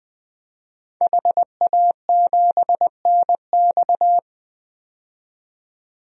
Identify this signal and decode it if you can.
Morse code, heard as a keyed tone, about 20 wpm, 700 Hz. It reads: HA7NX
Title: DXCC-100 Call Signs - 5 Characters - 20wpm